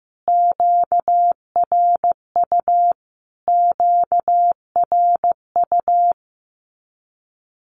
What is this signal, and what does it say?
Morse code, heard as a keyed tone, about 15 wpm, 700 Hz. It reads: QRU QRU
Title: Q-Codes Repeated 3x 15wpm